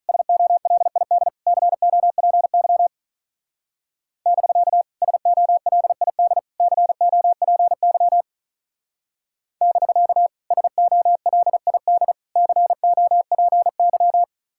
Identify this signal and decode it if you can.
Morse code, heard as a keyed tone, about 43 wpm, 700 Hz. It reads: SOLID COPY <BK> SOLID COPY <BK> SOLID COPY
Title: Speed-Racing Common QSO Phrases Encore - 35wpm